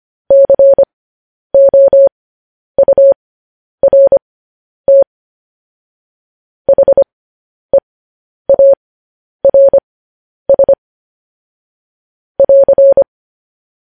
Morse code, heard as a keyed tone, about 25 wpm, spaced as for 10 wpm, 550 Hz. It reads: COURT HEARS <AR>